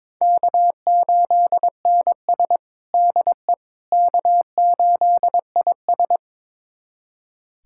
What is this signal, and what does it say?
Morse code, heard as a keyed tone, about 22 wpm, 700 Hz. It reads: K8NS DE K8IS
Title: USA Callsign DE Callsign - 4 Characters - 22wpm